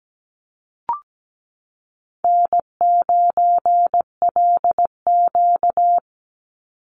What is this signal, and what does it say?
Morse code, heard as a keyed tone, about 17 wpm, 700 Hz. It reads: N9LQ
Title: Easy Call Signs Repeated 3x - 17wpm